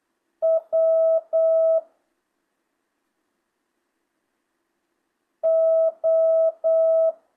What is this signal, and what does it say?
Morse code, heard as a keyed tone, about 8 wpm, 650 Hz. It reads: W O
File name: jh8cbh / 2025cw